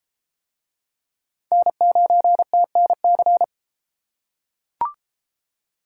Morse code, heard as a keyed tone, about 31 wpm, 700 Hz. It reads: N9TNC